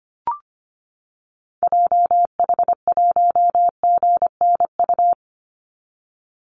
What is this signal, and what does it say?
Morse code, heard as a keyed tone, about 25 wpm, 700 Hz. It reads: JH1GNU
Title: Call Signs - 25wpm